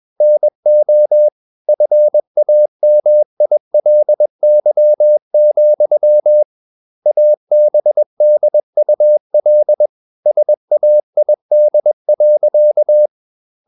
Morse code, heard as a keyed tone, about 21 wpm, 600 Hz. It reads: NO FAMILY, ABDUL SAID.
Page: CW med Gnister